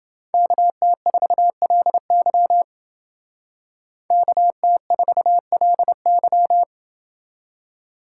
Morse code, heard as a keyed tone, about 29 wpm, 700 Hz. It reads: KT4LY KT4LY